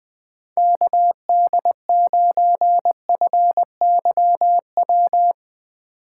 Morse code, heard as a keyed tone, about 20 wpm, 700 Hz. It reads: KD9FYW
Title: USA Call Signs - 6 Characters - 8x Character Spacing - 20wpm